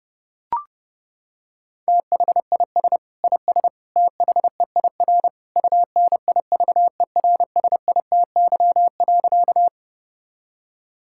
Morse code, heard as a keyed tone, about 29 wpm, 700 Hz. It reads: THIS IS THEIR UNIVERSITY.
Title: Sentences from Top 1000 Words - Review Part 5 - 30wpm